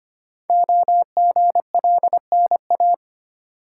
Morse code, heard as a keyed tone, about 25 wpm, 700 Hz. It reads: OGLNA
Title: Code Groups - 5 Letters - 25wpm